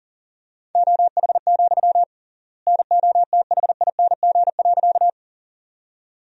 Morse code, heard as a keyed tone, about 40 wpm, 700 Hz. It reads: OH, NOTHING.
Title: 3x Word-Spacing - Sentences from Top 500 Words - 40wpm